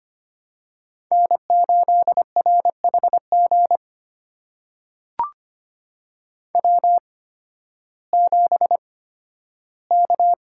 Morse code, heard as a keyed tone, about 25 wpm, spaced as for 13 wpm, 700 Hz. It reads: N8RHG W7K
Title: USA Call Signs - 8x Character Spacing - 25wpm